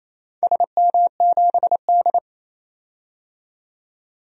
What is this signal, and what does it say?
Morse code, heard as a keyed tone, about 27 wpm, 690 Hz. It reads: SM7D
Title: DXCC-100 Call Signs - 4 Characters - 28wpm